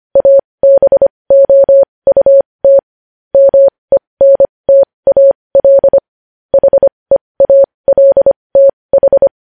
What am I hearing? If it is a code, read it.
Morse code, heard as a keyed tone, about 25 wpm, spaced as for 20 wpm, 550 Hz. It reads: ABOUT MENTAL HEALTH
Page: news-headlines-in-morse-code-at-20-wpm